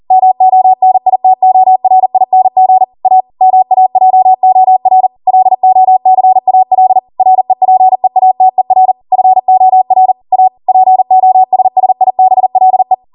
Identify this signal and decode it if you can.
Morse code, heard as a keyed tone, about 40 wpm, 750 Hz. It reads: MONITORING A MAJOR LOCAL REPEATER FOR A POSSIBLE
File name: Morse / 40WPM